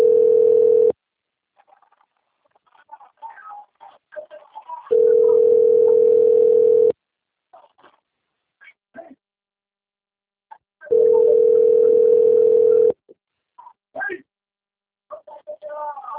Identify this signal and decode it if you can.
Ringback tone (North America), 440 + 480 Hz, with speech.
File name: out- - - - - .5289